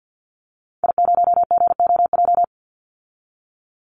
Morse code, heard as a keyed tone, about 50 wpm, 700 Hz. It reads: I0GOJ
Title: Call Signs - 50wpm